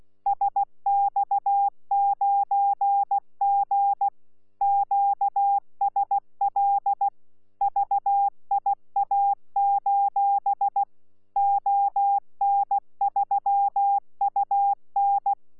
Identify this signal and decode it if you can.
Morse code, heard as a keyed tone, about 16 wpm, 800 Hz. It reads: SX9G QSL VIA: ON3UN
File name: Z PI4VRZ / PI4VRZ 14-jun-2025